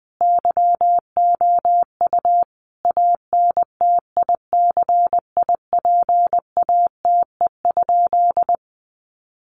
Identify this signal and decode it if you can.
Morse code, heard as a keyed tone, about 20 wpm, 700 Hz. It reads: YOU ANTICIPATE?
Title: Sentences from Top 1500 Words 20wpm